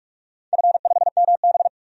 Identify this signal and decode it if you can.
Morse code, heard as a keyed tone, about 44 wpm, 700 Hz. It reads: F5MB
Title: DXCC-100 Call Signs - 4 Characters - 45wpm